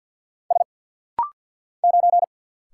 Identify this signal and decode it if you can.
Morse code, heard as a keyed tone, about 49 wpm, 700 Hz. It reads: S 9
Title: Single Letter-Number - Rapid-Fire 50wpm